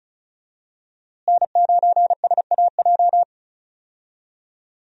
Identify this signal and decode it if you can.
Morse code, heard as a keyed tone, about 36 wpm, 700 Hz. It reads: N9SAJ
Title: USA Call Signs - 5 Characters - Repeated 3x - 35wpm